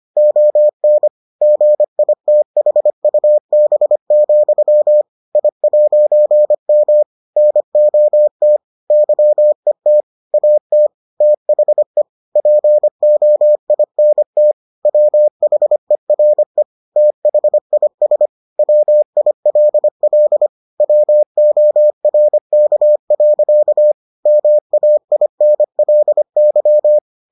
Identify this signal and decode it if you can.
Morse code, heard as a keyed tone, about 25 wpm, 600 Hz. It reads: ON GITHUB, I'M NOT YET AT THE POINT WHERE THIS WILL WORK. MAINLY